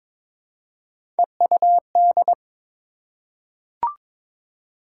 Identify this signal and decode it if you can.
Morse code, heard as a keyed tone, about 23 wpm, 700 Hz. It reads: EUD
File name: practice / 22wpm / Three Letter-Number Combinations - 22wpm